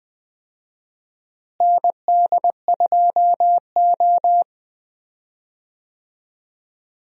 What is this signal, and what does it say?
Morse code, heard as a keyed tone, about 20 wpm, 700 Hz. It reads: ND2O
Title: USA Call Signs - 4 Characters - Repeated 2x - 20wpm